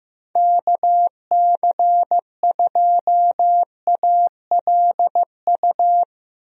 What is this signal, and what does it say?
Morse code, heard as a keyed tone, about 15 wpm, 700 Hz. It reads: KC2ALU
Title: USA Call Signs - 6 Characters - Repeated 2x - 15wpm